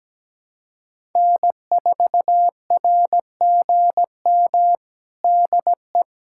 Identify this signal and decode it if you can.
Morse code, heard as a keyed tone, about 17 wpm, 700 Hz. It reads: N4RGM DE
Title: USA Callsign DE Callsign - 5 Characters - 17wpm